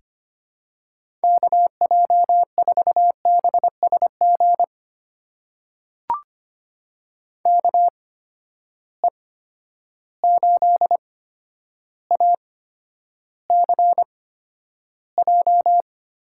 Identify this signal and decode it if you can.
Morse code, heard as a keyed tone, about 25 wpm, spaced as for 11 wpm, 700 Hz. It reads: KJ4BSG KE8ACJ
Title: USA Call Signs - 6 Characters - 8x Character Spacing - 25wpm